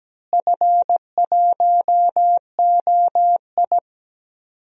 Morse code, heard as a keyed tone, about 17 wpm, 700 Hz. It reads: F1OI
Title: DXCC-100 Call Signs - 4 Characters - Repeated 3x - 17wpm